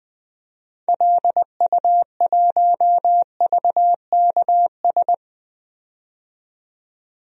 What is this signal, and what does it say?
Morse code, heard as a keyed tone, about 20 wpm, 700 Hz. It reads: LU1VKS
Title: DXCC-100 Call Signs - 6 Characters - 20wpm